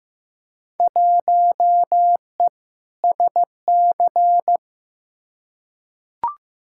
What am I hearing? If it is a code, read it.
Morse code, heard as a keyed tone, about 15 wpm, 700 Hz. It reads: 1E SC
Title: Field Day Exchanges - 15wpm